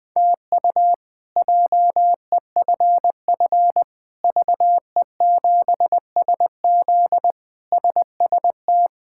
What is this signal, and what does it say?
Morse code, heard as a keyed tone, about 20 wpm, 700 Hz. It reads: TU JEFF VE7SZ SST
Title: K1USN SST QSO Practice - 20wpm